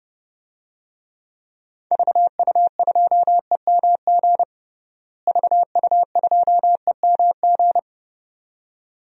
Morse code, heard as a keyed tone, about 30 wpm, 700 Hz. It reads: VU2EMG VU2EMG
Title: DXCC-100 Call Signs - 6 Characters - Repeated 2x - 30wpm